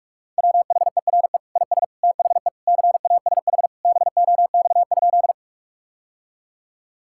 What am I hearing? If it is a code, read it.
Morse code, heard as a keyed tone, about 46 wpm, 700 Hz. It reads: WHERE IS THE CASH BOX?
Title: Sentences from Top 1000 Words - Review Part 5 - 45wpm